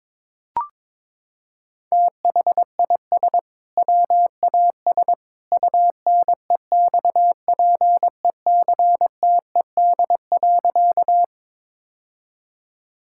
Morse code, heard as a keyed tone, about 22 wpm, 700 Hz. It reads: THIS WAS UNEXPECTED.
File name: practice / 22wpm / Sentences with 2-Word Phrases - 22wpm